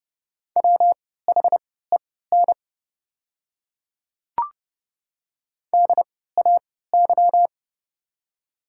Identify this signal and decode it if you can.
Morse code, heard as a keyed tone, about 30 wpm, spaced as for 17 wpm, 700 Hz. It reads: WHEN DAY
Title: 3x Character Spacing - Top 200 Words - 30wpm